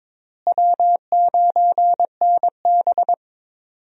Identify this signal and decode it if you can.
Morse code, heard as a keyed tone, about 21 wpm, 700 Hz. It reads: W9NB
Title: USA Call Signs - 4 Characters - Repeated 2x - 22wpm